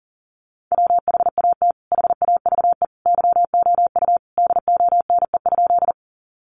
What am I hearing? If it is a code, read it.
Morse code, heard as a keyed tone, about 40 wpm, 700 Hz. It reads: WHAT HAVE YOU DONE?
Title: Sentences from Top 500 Words 40wpm